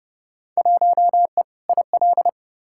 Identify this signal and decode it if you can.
Morse code, heard as a keyed tone, about 30 wpm, 700 Hz. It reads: 1E IL
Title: Field Day Exchanges - 30wpm